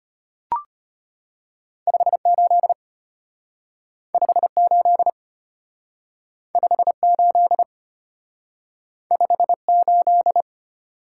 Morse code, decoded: 58 58 58 58

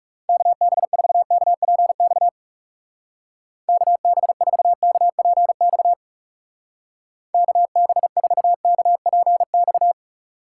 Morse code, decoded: KB4KPX KB4KPX KB4KPX

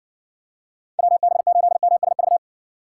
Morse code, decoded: WB8MSV

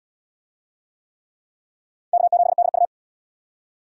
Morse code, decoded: K7DU

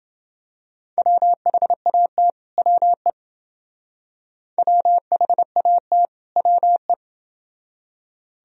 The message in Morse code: WHAT WE WHAT WE

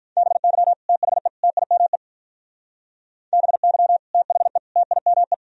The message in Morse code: BY THE TIME BY THE TIME

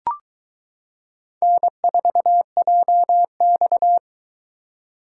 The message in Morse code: N4JX

